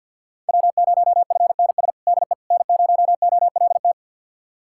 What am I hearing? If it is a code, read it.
Morse code, heard as a keyed tone, about 51 wpm, 700 Hz. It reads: W0FNS DE N0OLT
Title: USA Callsign DE Callsign - 5 Characters - 50wpm